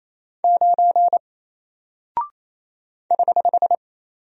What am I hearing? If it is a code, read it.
Morse code, heard as a keyed tone, about 27 wpm, 700 Hz. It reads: 9 <HH>
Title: Single Character (Letter:Number:Punctuation:Prosign) Alt 28wpm